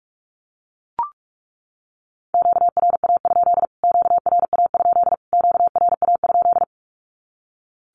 Morse code, heard as a keyed tone, about 44 wpm, 700 Hz. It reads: QRA? QRA? QRA?
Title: Q-Codes Repeated 3x 45wpm